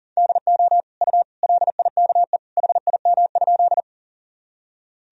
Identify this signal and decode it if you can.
Morse code, heard as a keyed tone, about 38 wpm, 700 Hz. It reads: DO U LIKE HIM?
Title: Sentences from Top 100 Words and Abbreviations 40wpm